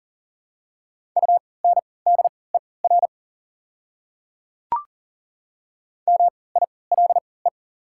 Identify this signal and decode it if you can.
Morse code, heard as a keyed tone, about 40 wpm, spaced as for 23 wpm, 700 Hz. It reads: UNDER MILE
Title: 3x Character Spacing - Top 300 Words - 40wpm